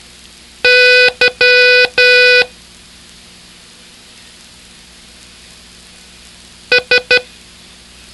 Morse code, heard as a keyed tone, about 11 wpm, 500 Hz. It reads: Y S